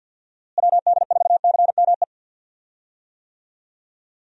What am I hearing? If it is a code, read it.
Morse code, heard as a keyed tone, about 48 wpm, 700 Hz. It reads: WD4CME
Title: USA Call Signs - 6 Characters - Repeated 3x - 50wpm